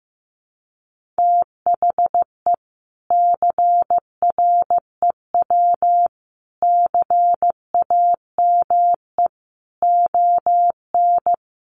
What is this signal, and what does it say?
Morse code, heard as a keyed tone, about 15 wpm, 700 Hz. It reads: THE CREW CAME ON
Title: Sentences from Top 1500 Words 15wpm